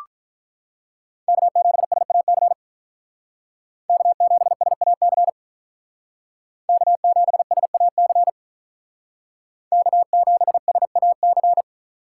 Morse code, heard as a keyed tone, about 44 wpm, 700 Hz. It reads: K7SAC K7SAC K7SAC K7SAC